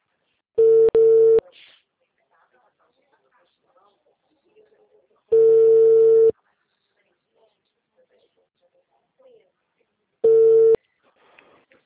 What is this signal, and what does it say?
Ringback tone (China, 450 Hz), with speech.